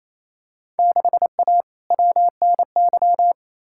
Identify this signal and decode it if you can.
Morse code, heard as a keyed tone, about 28 wpm, 700 Hz. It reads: 6A WNY
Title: Field Day Exchanges Encore - 28wpm